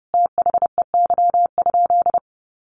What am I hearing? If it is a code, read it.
Morse code, heard as a keyed tone, about 30 wpm, 700 Hz. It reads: THEY?
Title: Sentences from Top 200 Words 30wpm